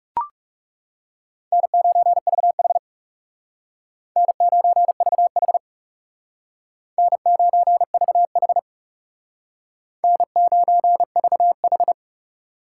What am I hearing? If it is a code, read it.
Morse code, heard as a keyed tone, about 40 wpm, 700 Hz. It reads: N9VH N9VH N9VH N9VH